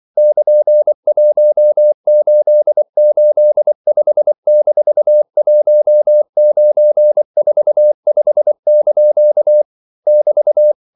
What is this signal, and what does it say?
Morse code, heard as a keyed tone, about 24 wpm, 600 Hz. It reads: <KN>1885-1945) <BT>
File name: CW / Practice / EnglishQuotations0324